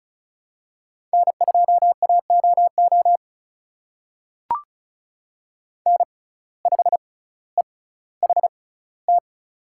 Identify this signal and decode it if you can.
Morse code, heard as a keyed tone, about 36 wpm, spaced as for 18 wpm, 700 Hz. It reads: N2AOO N5EHT
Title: USA Call Signs - 5 Characters - 6x Character Spacing - 35wpm